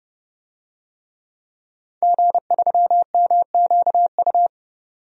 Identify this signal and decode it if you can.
Morse code, heard as a keyed tone, about 31 wpm, 700 Hz. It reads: G3MQU
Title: DXCC-100 Call Signs - 5 Characters - 30wpm